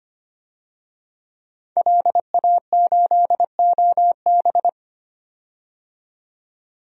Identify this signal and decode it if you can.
Morse code, heard as a keyed tone, about 25 wpm, 700 Hz. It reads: LA8OB
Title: DXCC-100 Call Signs - 5 Characters - 25wpm